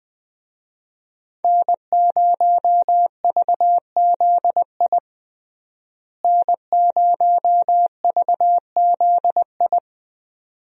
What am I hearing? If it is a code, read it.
Morse code, heard as a keyed tone, about 20 wpm, 700 Hz. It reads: N0VZI N0VZI
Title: USA Call Signs - 5 Characters - Repeated 2x - 20wpm